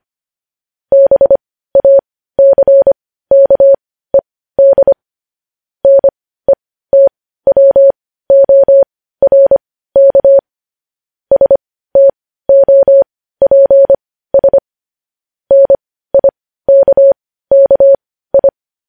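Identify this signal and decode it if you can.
Morse code, heard as a keyed tone, about 25 wpm, spaced as for 15 wpm, 550 Hz. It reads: BACKED NETWORK STOPS NIKKI
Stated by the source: Morse code news headlines at 15.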